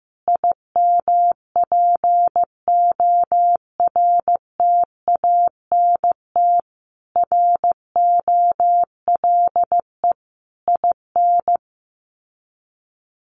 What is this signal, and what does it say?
Morse code, heard as a keyed tone, about 15 wpm, 700 Hz. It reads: IMPORTANT ROLE IN
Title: Sets of 5 Words Encore2 15wpm